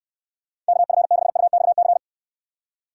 Morse code, heard as a keyed tone, about 69 wpm, 700 Hz. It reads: KF6UXC